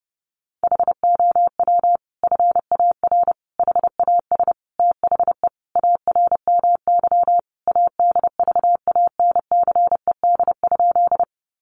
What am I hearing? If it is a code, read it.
Morse code, heard as a keyed tone, about 30 wpm, 700 Hz. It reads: HOW FAR HAS THE ARMY ADVANCED?
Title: Sentences from Top 1200 Words 30wpm